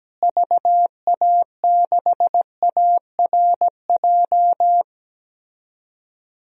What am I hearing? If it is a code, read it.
Morse code, heard as a keyed tone, about 17 wpm, 700 Hz. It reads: VA6ARJ
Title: DXCC-100 Call Signs - 6 Characters - Repeated 2x - 17wpm